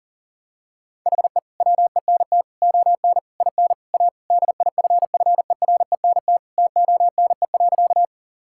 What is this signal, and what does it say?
Morse code, heard as a keyed tone, about 40 wpm, 700 Hz. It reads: HE WENT ON IN A DIFFERENT TONE.
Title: Sentences from Top 1600 Words 40wpm